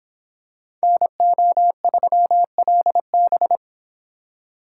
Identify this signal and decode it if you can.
Morse code, heard as a keyed tone, about 26 wpm, 700 Hz. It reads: NO3LB